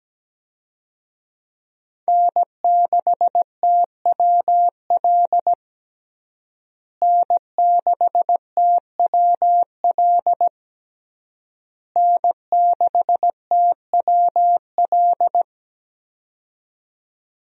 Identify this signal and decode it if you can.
Morse code, heard as a keyed tone, about 17 wpm, 700 Hz. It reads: N6TWL N6TWL N6TWL
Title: USA Call Signs - 5 Characters - Repeated 3x - 17wpm